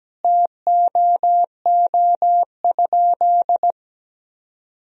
Morse code, heard as a keyed tone, about 17 wpm, 700 Hz. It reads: TOO?
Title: Sentences from Top 500 Words and Abbreviations 17wpm